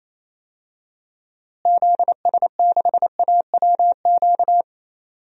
Morse code, heard as a keyed tone, about 27 wpm, 700 Hz. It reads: ZS6AWQ
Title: DXCC-100 Call Signs - 6 Characters - Repeated 3x - 28wpm